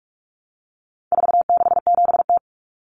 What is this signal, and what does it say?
Morse code, heard as a keyed tone, about 44 wpm, 700 Hz. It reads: 467T
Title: Common Cut Numbers - 4-Digit 45wpm